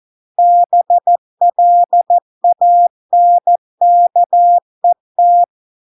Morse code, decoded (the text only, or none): BLANKET